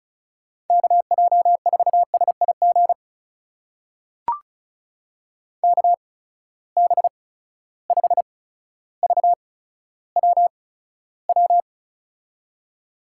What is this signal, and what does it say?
Morse code, heard as a keyed tone, about 36 wpm, spaced as for 15 wpm, 700 Hz. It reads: KJ4SIG KB5VWW
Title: USA Call Signs - 6 Characters - 8x Character Spacing - 35wpm